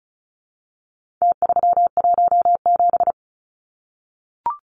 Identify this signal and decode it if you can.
Morse code, heard as a keyed tone, about 34 wpm, 700 Hz. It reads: T317
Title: Numbers - Contest Style - 4-Digit 35wpm